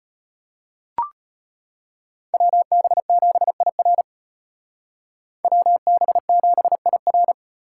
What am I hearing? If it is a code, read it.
Morse code, heard as a keyed tone, about 31 wpm, 700 Hz. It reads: WB7IR WB7IR
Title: Speed Racing - USA 2x2 Call Signs 25wpm